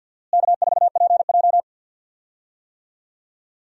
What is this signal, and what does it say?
Morse code, heard as a keyed tone, about 51 wpm, 700 Hz. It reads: K4PJ
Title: USA Call Signs - 4 Characters - Repeated 3x - 50wpm